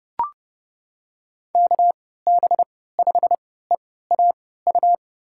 Morse code, heard as a keyed tone, about 30 wpm, spaced as for 17 wpm, 700 Hz. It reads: KB5EAU
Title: USA Call Signs - 3x Character Spacing - 30wpm